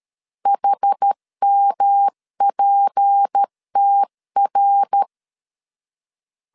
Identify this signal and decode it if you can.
Morse code, heard as a keyed tone, about 13 wpm, 800 Hz. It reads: HMPTR
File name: CW / DET / 1 6 1500C S C 12wpm 3533